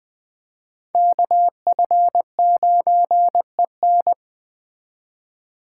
Morse code, heard as a keyed tone, about 20 wpm, 700 Hz. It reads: KF9EN